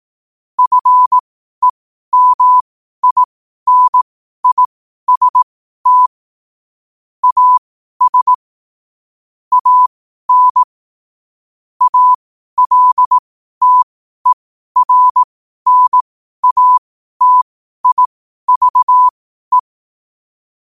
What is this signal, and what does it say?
Morse code, heard as a keyed tone, about 18 wpm, spaced as for 12 wpm, 1000 Hz. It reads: FEMINIST AS AN ALTERNATIVE
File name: qotdfiles / 12wpm